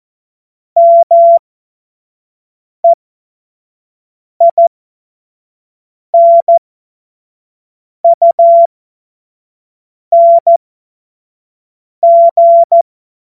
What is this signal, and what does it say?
Morse code, heard as a keyed tone, about 14 wpm, spaced as for 5 wpm, 680 Hz. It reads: MEINUNG
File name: Morsekurs / Witze / Witz 13. Juni 2023 25 F70